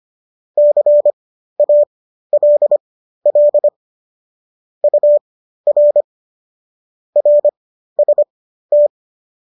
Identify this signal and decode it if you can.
Morse code, heard as a keyed tone, about 25 wpm, spaced as for 13 wpm, 590 Hz. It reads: CALL UR RST
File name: qso206 13